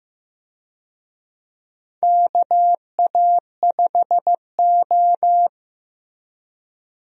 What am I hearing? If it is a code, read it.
Morse code, heard as a keyed tone, about 15 wpm, 700 Hz. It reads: KA5O
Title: USA Call Signs - 4 Characters - Repeated 2x - 15wpm